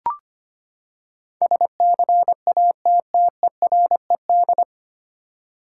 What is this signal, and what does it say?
Morse code, heard as a keyed tone, about 25 wpm, 700 Hz. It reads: SCATTERED
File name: Nine Letter Words - 25wpm